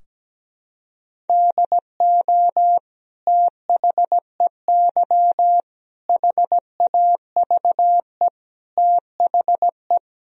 Morse code, heard as a keyed tone, about 17 wpm, 700 Hz. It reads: DO THEY HAVE THE
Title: Sentences from Top 1000 Words - Review Part 3 - 17wpm